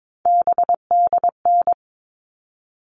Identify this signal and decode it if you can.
Morse code, heard as a keyed tone, about 22 wpm, 700 Hz. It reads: BDN
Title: All Cut Numbers - 3-Digit 22wpm